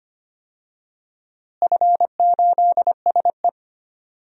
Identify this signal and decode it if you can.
Morse code, heard as a keyed tone, about 25 wpm, 700 Hz. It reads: F8SE
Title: DXCC-100 Call Signs - 4 Characters - Repeated 2x - 25wpm